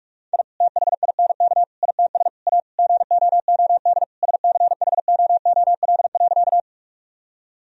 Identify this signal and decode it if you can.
Morse code, heard as a keyed tone, about 46 wpm, 700 Hz. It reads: I THINK ITS A GOOD SCHOOL.
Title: Sentences from Top 1000 Words - Review Part 1 - 45wpm